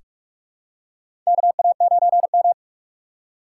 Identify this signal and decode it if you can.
Morse code, heard as a keyed tone, about 46 wpm, 700 Hz. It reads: KA9M
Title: USA Call Signs - 4 Characters - 2x Character Spacing - 45wpm